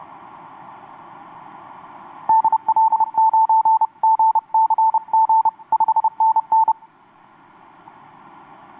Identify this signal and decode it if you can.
Morse code, heard as a keyed tone, about 29 wpm, 890 Hz. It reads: DL9GCG 5NN
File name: xw2cw rx6am etc 12m cw 26nov2012 1153